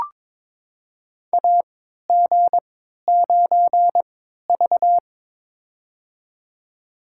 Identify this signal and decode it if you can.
Morse code, heard as a keyed tone, about 23 wpm, spaced as for 13 wpm, 700 Hz. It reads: AG9V
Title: USA Call Signs - 4 Characters - 3x Character Spacing - 22wpm